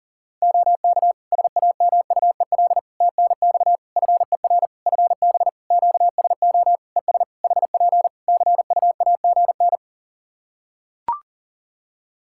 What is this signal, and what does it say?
Morse code, heard as a keyed tone, about 40 wpm, 700 Hz. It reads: OK SAMUEL TNX FER FB QSO ES HP CUAGN
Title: K1USN SST QSO Practice 40wpm